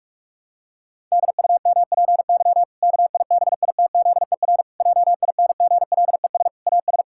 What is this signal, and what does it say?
Morse code, heard as a keyed tone, about 46 wpm, 700 Hz. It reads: DUMPY KIBITZER JINGLES AS
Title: Pangram Challenge v2 - 45wpm